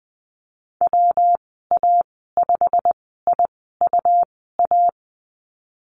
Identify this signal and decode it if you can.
Morse code, heard as a keyed tone, about 20 wpm, spaced as for 15 wpm, 700 Hz. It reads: WA5IUA
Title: 147 Lesson 24g - Letter y - Call Signs 20wpm